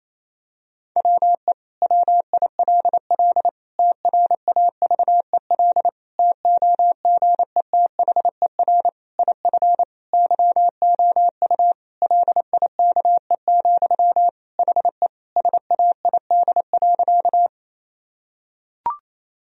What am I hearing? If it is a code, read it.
Morse code, heard as a keyed tone, about 28 wpm, 700 Hz. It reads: WE WILL TRAVEL TOGETHER IF YOU LIKE, HE SAID.
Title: Sentences from Top 1000 Words - Review Part 2 - 28wpm